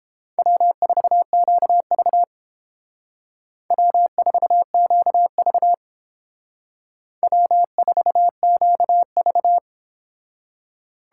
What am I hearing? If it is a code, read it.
Morse code, heard as a keyed tone, about 29 wpm, 700 Hz. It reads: W4QV W4QV W4QV